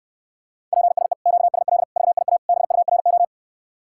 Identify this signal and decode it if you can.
Morse code, heard as a keyed tone, about 69 wpm, 700 Hz. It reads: PSE QSL VIA BURO